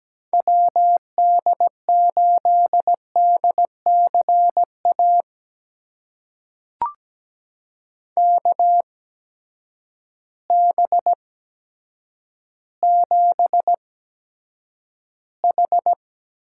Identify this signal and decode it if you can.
Morse code, heard as a keyed tone, about 17 wpm, spaced as for 8 wpm, 700 Hz. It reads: WD8DCA KB7H
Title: USA Call Signs - 6 Characters - 8x Character Spacing - 17wpm